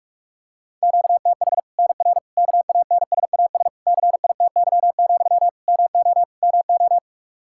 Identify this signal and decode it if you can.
Morse code, heard as a keyed tone, about 46 wpm, 700 Hz. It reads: QTH NR KANSAS CITY, MO MO
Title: Practice QSO - Exchange 1 - Beginning - 45wpm